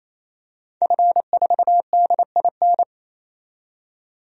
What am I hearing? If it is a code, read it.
Morse code, heard as a keyed tone, about 27 wpm, 700 Hz. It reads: F4DIN
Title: DXCC-100 Call Signs - 5 Characters - 28wpm